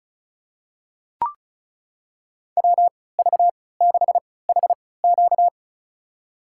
Morse code, heard as a keyed tone, about 34 wpm, spaced as for 20 wpm, 700 Hz. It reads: WV6HQ
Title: USA Call Signs - 5 Characters - 3x Character Spacing - 35wpm